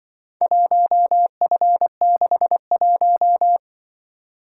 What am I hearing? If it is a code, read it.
Morse code, heard as a keyed tone, about 25 wpm, 700 Hz. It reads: 1F61
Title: Speed-Racing - Four Letter-Number Combinations - 20wpm